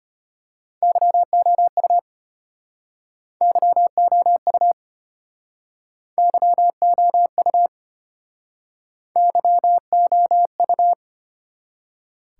Morse code, heard as a keyed tone, about 32 wpm, 700 Hz. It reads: YOU YOU YOU YOU